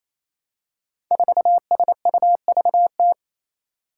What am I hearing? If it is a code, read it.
Morse code, heard as a keyed tone, about 29 wpm, 700 Hz. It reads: 4SUVT
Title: Five Challenging Letter-Number Combinations 28wpm